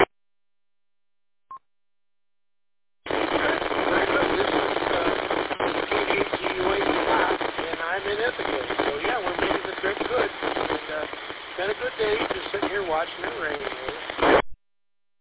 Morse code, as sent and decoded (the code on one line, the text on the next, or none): none